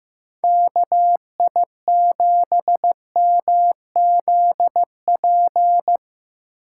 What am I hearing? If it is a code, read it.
Morse code, heard as a keyed tone, about 15 wpm, 700 Hz. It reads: KI7MZP
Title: USA Call Signs - 6 Characters - 3x Character Spacing - 15wpm